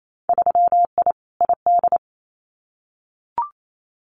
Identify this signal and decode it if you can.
Morse code, heard as a keyed tone, about 28 wpm, 700 Hz. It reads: 3I ID
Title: Winter Field Day Exchanges 28wpm